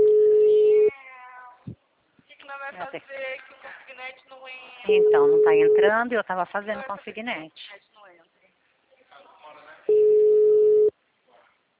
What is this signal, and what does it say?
Ringback tone (Europe, 425 Hz), with speech.